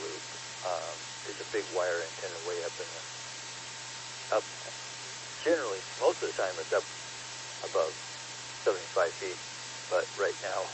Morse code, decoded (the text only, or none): none